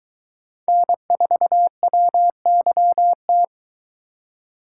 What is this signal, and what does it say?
Morse code, heard as a keyed tone, about 22 wpm, 700 Hz. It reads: N4WYT